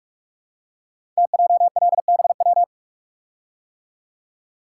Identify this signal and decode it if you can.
Morse code, heard as a keyed tone, about 44 wpm, 690 Hz. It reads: TJLBW
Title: Code Groups - 5 Letter:Numbers - 45wpm